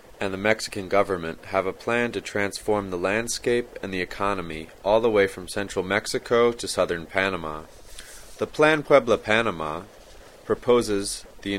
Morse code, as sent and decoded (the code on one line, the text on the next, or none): none